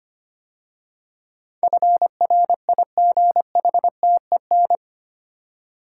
Morse code, heard as a keyed tone, about 25 wpm, 700 Hz. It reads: FRIGHTEN
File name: 8 Letter Words - First 3 Letters Given - 25wpm